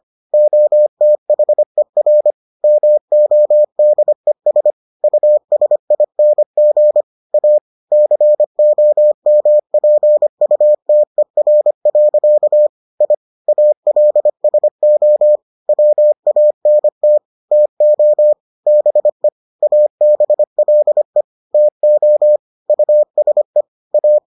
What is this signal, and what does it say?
Morse code, heard as a keyed tone, about 25 wpm, 600 Hz. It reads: OTHER MODES USING A COMPUTER. I ALSO WANT TO BE ABLE TO USE A